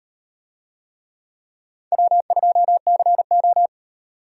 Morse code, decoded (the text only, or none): W2CO